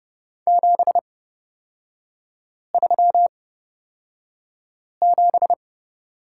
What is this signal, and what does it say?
Morse code, heard as a keyed tone, about 30 wpm, spaced as for 5 wpm, 700 Hz. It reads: 737
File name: Lesson 36e - BT - Three Characters 30wpm